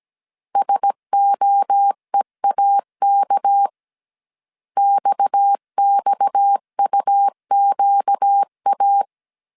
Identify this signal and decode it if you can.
Morse code, heard as a keyed tone, about 17 wpm, 800 Hz. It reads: SOEAK XXUQA
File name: 16wpm